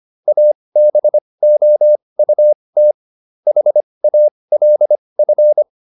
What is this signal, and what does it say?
Morse code, heard as a keyed tone, about 25 wpm, spaced as for 20 wpm, 600 Hz. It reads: ABOUT HALF